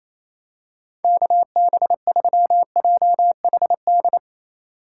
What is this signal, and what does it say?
Morse code, heard as a keyed tone, about 28 wpm, 690 Hz. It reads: KB3JHD
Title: USA Call Signs - 6 Characters - 4x Character Spacing - 28wpm